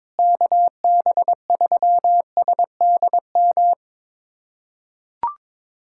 Morse code, heard as a keyed tone, about 22 wpm, 700 Hz. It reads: KB3SDM
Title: USA Callsign DE Callsign - 6 Characters - 22wpm